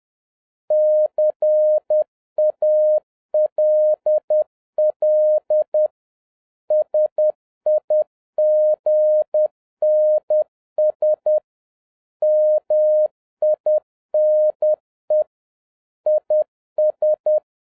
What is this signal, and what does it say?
Morse code, heard as a keyed tone, about 10 wpm, 610 Hz. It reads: CALL SIGNS MINE IS